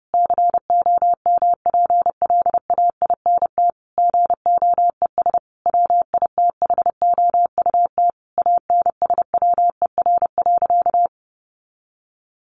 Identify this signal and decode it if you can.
Morse code, heard as a keyed tone, about 28 wpm, 700 Hz. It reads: COMPLAINT GOES WITHOUT ANSWER.
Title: Sentences from Top 1500 Words 30wpm